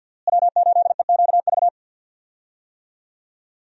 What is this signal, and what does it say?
Morse code, heard as a keyed tone, about 52 wpm, 700 Hz. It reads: W8EQV
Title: USA Call Signs - 5 Characters - Repeated 3x - 50wpm